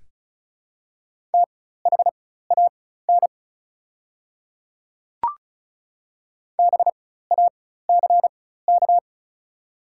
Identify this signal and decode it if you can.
Morse code, heard as a keyed tone, about 34 wpm, spaced as for 16 wpm, 700 Hz. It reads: THAN BACK
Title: 4x Character Spacing - Top 200 Words - 35wpm